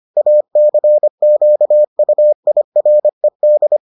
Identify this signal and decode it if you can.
Morse code, heard as a keyed tone, about 25 wpm, 600 Hz. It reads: ACQUIRED